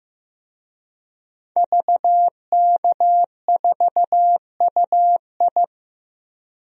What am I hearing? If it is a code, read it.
Morse code, heard as a keyed tone, about 15 wpm, 700 Hz. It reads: VK4UI
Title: DXCC-100 Call Signs - 5 Characters - Repeated 2x - 15wpm